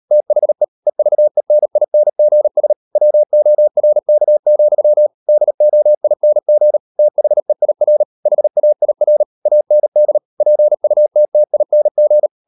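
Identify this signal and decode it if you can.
Morse code, heard as a keyed tone, about 38 wpm, 600 Hz. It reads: THE EVENINGS WORK, DOING THEIR HAIR AND PUTTING
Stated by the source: Morse code practice at 38 WPM, sending Ken Follett's novel Never in English, from CW med Gnister.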